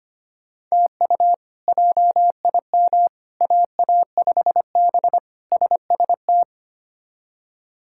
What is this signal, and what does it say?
Morse code, heard as a keyed tone, about 25 wpm, 700 Hz. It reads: TU JIM AA5B SST
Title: K1USN SST QSO Practice - 25wpm